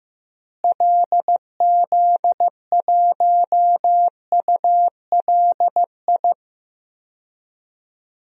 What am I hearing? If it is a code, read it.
Morse code, heard as a keyed tone, about 15 wpm, 700 Hz. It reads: LZ1ULI
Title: DXCC-100 Call Signs - 6 Characters - Repeated 2x - 15wpm